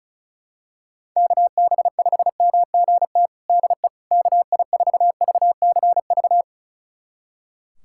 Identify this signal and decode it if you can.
Morse code, heard as a keyed tone, about 34 wpm, 700 Hz. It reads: KB5MGT DE KI4VCV
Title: USA Callsign DE Callsign - 6 Characters - 35wpm